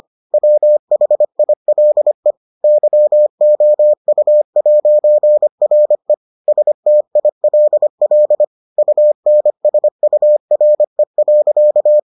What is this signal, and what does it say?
Morse code, heard as a keyed tone, about 25 wpm, 600 Hz. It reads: WHILE YOU'RE STILL UNSURE.